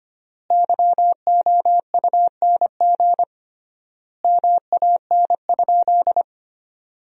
Morse code, heard as a keyed tone, about 25 wpm, 700 Hz. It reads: YOUNG MAN?
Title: 3x Word-Spacing - Sentences from Top 400 Words - 25wpm